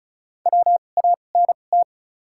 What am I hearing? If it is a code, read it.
Morse code, heard as a keyed tone, about 34 wpm, spaced as for 25 wpm, 700 Hz. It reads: WANT
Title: 2x Character Spacing - Top 300 Words - 35wpm